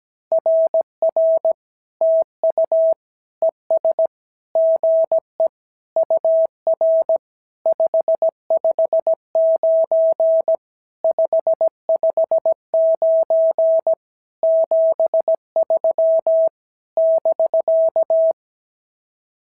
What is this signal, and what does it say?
Morse code, heard as a keyed tone, about 17 wpm, 650 Hz. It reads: RR TU ES GE UR 559 559 73 <BK>
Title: SOTA QSO Practice 17wpm